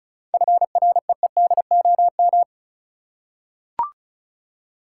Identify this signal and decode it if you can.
Morse code, heard as a keyed tone, about 36 wpm, 700 Hz. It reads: FREEDOM
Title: Seven Letter Words - 35wpm